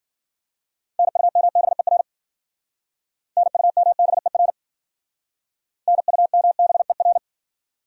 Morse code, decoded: NUMBER NUMBER NUMBER

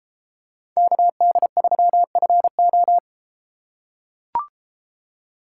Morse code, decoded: KD3FO

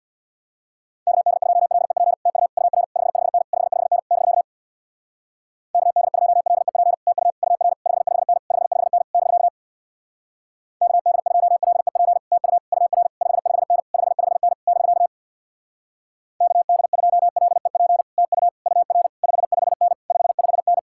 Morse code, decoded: KD2LEP TU UR 55N 55N <BK> KD2LEP TU UR 55N 55N <BK> KD2LEP TU UR 55N 55N <BK> KD2LEP TU UR 55N 55N

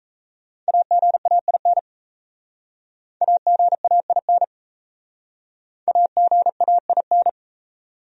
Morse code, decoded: AGAIN AGAIN AGAIN

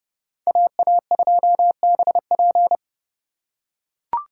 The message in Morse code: AA2BP

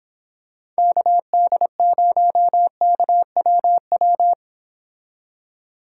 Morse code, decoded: KD0KWW

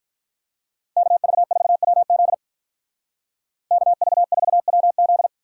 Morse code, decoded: KV4WZ KV4WZ

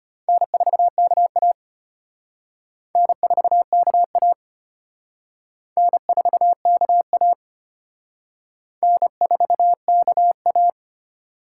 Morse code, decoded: N4KA N4KA N4KA N4KA

